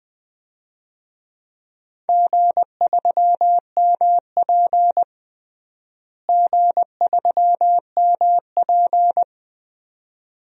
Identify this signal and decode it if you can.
Morse code, heard as a keyed tone, about 21 wpm, 700 Hz. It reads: G3MP G3MP
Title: DXCC-100 Call Signs - 4 Characters - Repeated 2x - 20wpm